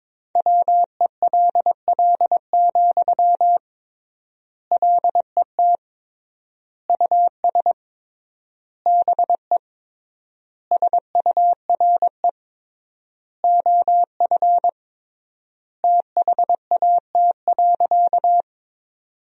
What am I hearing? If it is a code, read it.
Morse code, heard as a keyed tone, about 21 wpm, 700 Hz. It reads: WELL, LET US BE SURE OF THAT.
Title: 3x Word-Spacing - Sentences from Top 300 Words - 22wpm